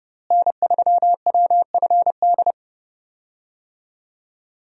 Morse code, heard as a keyed tone, about 30 wpm, 700 Hz. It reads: N3WFD
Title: USA Call Signs - 5 Characters - Repeated 2x - 30wpm